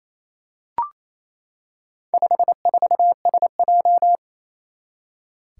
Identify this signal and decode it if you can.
Morse code, heard as a keyed tone, about 27 wpm, 700 Hz. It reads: 54SJ
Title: Four Challenging Letter-Number Combinations 28wpm